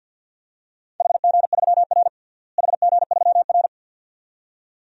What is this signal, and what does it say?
Morse code, heard as a keyed tone, about 47 wpm, 700 Hz. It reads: HG3R HG3R
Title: DXCC-100 Call Signs - 4 Characters - Repeated 2x - 50wpm